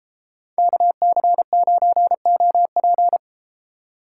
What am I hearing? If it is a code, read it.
Morse code, heard as a keyed tone, about 33 wpm, 700 Hz. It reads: KC9OP